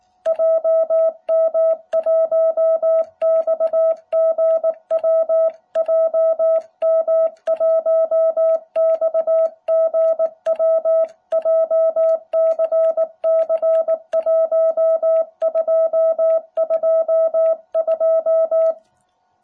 Morse code, heard as a keyed tone, about 19 wpm, 650 Hz. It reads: JM1XGW JM1XGW JCC1222